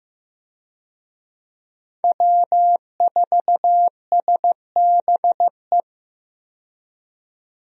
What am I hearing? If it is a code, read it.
Morse code, heard as a keyed tone, about 16 wpm, 700 Hz. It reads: W4SBE
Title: USA Call Signs - 5 Characters - 15wpm